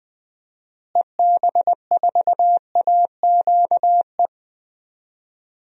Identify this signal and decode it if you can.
Morse code, heard as a keyed tone, about 20 wpm, 700 Hz. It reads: EB4AQE
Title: DXCC-100 Call Signs - 6 Characters - Repeated 2x - 20wpm